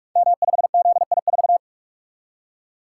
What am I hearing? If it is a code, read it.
Morse code, heard as a keyed tone, about 44 wpm, 700 Hz. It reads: M5ZI4
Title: Code Groups - 5 Letter:Numbers - 45wpm